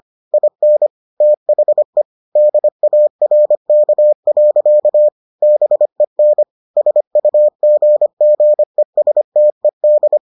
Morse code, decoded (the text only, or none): IN THE DARK. BEN SUGGESTED